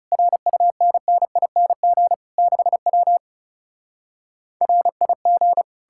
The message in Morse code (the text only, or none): RUNNING 6W RIG